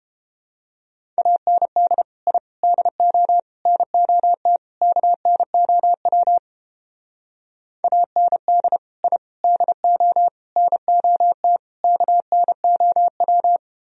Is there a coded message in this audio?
Yes: Morse code from 1.2 to 13.6 s, heard as a keyed tone, about 29 wpm, 700 Hz. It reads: AND I DO NOT KNOW AND I DO NOT KNOW